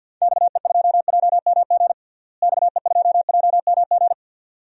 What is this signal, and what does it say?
Morse code, heard as a keyed tone, about 50 wpm, 700 Hz. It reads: XE2JMG XE2JMG
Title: DXCC-100 Call Signs - 6 Characters - Repeated 2x - 50wpm